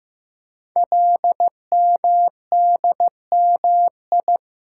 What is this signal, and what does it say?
Morse code, heard as a keyed tone, about 15 wpm, 700 Hz. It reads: LMDMI
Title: Code Groups - 5 Letters - 15wpm